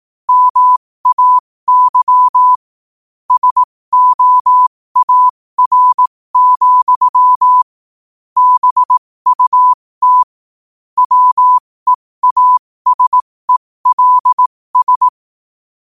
Morse code, heard as a keyed tone, about 18 wpm, spaced as for 15 wpm, 1000 Hz. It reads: MAY SOAR, BUT WEASELS
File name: qotdfiles / 15wpm